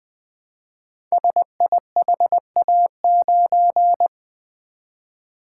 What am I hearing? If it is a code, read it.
Morse code, heard as a keyed tone, about 20 wpm, 700 Hz. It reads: SIHA9
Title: Code Groups - 5 Letter:Numbers - 20wpm